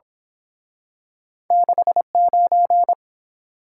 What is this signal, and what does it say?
Morse code, heard as a keyed tone, about 27 wpm, 700 Hz. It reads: 69
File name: Speed-Racing - Two Letter-Number Combinations - 22wpm